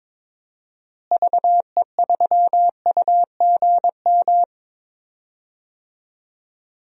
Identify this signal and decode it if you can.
Morse code, heard as a keyed tone, about 22 wpm, 700 Hz. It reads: VE3UGM
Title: DXCC-100 Call Signs - 6 Characters - Repeated 2x - 22wpm